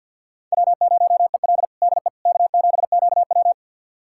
Morse code, heard as a keyed tone, about 51 wpm, 700 Hz. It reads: W0EL DE K7QW
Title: USA Callsign DE Callsign - 4 Characters - 50wpm